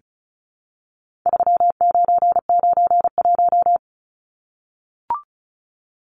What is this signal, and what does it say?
Morse code, heard as a keyed tone, about 34 wpm, 700 Hz. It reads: 3991
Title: Numbers - 4-Digit 35wpm